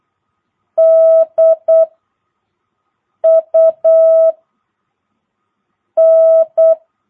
Morse code, heard as a keyed tone, about 8 wpm, 660 Hz. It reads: D U N
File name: jh8cbh / 2022cw